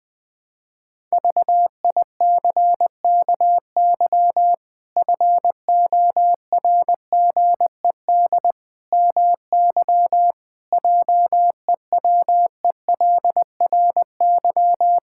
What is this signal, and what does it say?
Morse code, heard as a keyed tone, about 21 wpm, 700 Hz. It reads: VICKY FORGED MY JEWELRY
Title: Pangram Challenge v2 - 20wpm